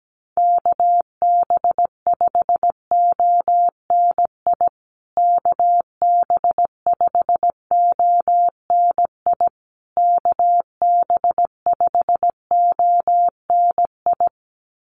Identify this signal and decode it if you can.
Morse code, heard as a keyed tone, about 17 wpm, 700 Hz. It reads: KB5ONI KB5ONI KB5ONI
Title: USA Call Signs Repeated 3x 17wpm